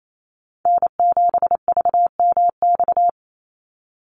Morse code, heard as a keyed tone, about 28 wpm, 700 Hz. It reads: N7VMX
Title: USA Call Signs 28wpm